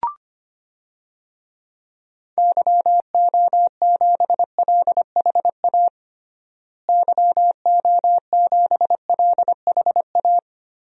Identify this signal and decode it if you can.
Morse code, heard as a keyed tone, about 25 wpm, 700 Hz. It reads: YO7LHA YO7LHA
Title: DXCC-100 Call Signs - 6 Characters - Repeated 3x - 25wpm